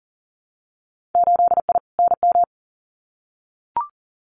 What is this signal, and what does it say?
Morse code, heard as a keyed tone, about 40 wpm, 700 Hz. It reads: 8I NM
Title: Winter Field Day Exchanges Encore 40wpm